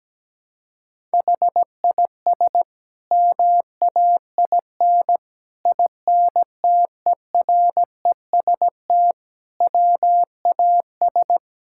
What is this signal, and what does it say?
Morse code, heard as a keyed tone, about 17 wpm, 700 Hz. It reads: HIS MAIN INTEREST WAS
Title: Sentences from Top 1000 Words - Review Part 6 - 17wpm